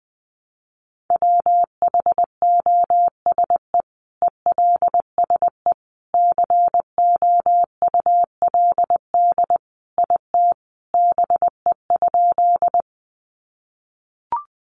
Morse code, heard as a keyed tone, about 20 wpm, 700 Hz. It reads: WHOSE ELSE COULD IT BE?
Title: Sentences from Top 1200 Words 20wpm